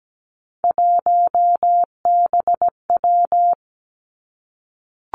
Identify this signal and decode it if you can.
Morse code, heard as a keyed tone, about 17 wpm, 700 Hz. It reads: 1BW
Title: Three Challenging Letter-Number Combinations 17wpm